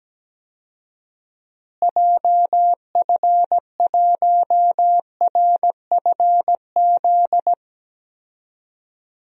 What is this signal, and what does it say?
Morse code, heard as a keyed tone, about 17 wpm, 700 Hz. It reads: JF1RFZ
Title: DXCC-100 Call Signs - 6 Characters - Repeated 2x - 17wpm